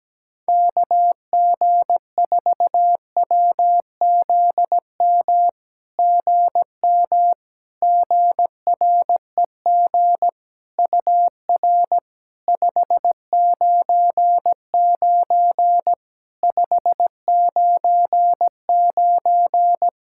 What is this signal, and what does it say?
Morse code, heard as a keyed tone, about 17 wpm, 700 Hz. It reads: KG4WZM GM GREG UR 599 599
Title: SOTA QSO Practice 17wpm